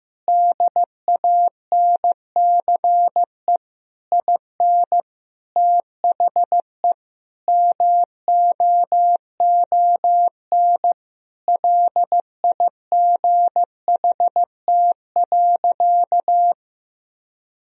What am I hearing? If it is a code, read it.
Morse code, heard as a keyed tone, about 15 wpm, 700 Hz. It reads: DANCE IN THE MOON LIGHT.
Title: Sentences from Top 1000 Words - Review Part 4 - 15wpm